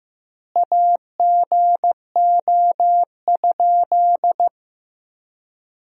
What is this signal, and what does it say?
Morse code, heard as a keyed tone, about 15 wpm, 700 Hz. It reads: AGO?
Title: 4x Word-Spacing - Sentences from Top 500 Words - 15wpm